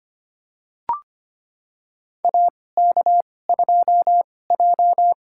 Morse code, heard as a keyed tone, about 25 wpm, spaced as for 18 wpm, 700 Hz. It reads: AK2J
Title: USA Call Signs - 4 Characters - 2x Character Spacing - 25wpm